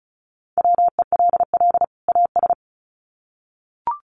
Morse code, heard as a keyed tone, about 34 wpm, 700 Hz. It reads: WELL AS
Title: Sets of 2 Words 35wpm